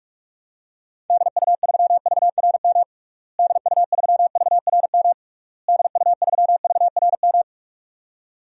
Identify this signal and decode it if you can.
Morse code, heard as a keyed tone, about 46 wpm, 700 Hz. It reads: DU3VRM DU3VRM DU3VRM
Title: DXCC-100 Call Signs - 6 Characters - Repeated 3x - 45wpm